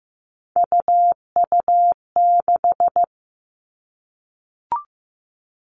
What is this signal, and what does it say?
Morse code, heard as a keyed tone, about 15 wpm, 700 Hz. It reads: UU6
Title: Three Challenging Letter-Number Combinations 15wpm